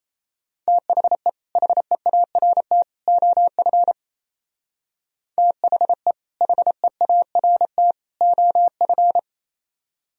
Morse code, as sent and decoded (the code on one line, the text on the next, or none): - .... / .... .- .-. - / --- ..-. / - .... / .... .- .-. - / --- ..-.
THE HEART OF THE HEART OF